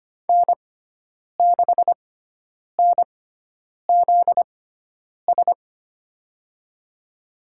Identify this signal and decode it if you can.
Morse code, heard as a keyed tone, about 25 wpm, spaced as for 9 wpm, 700 Hz. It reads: N6NZS
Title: USA Call Signs - 6x Character Spacing - 25wpm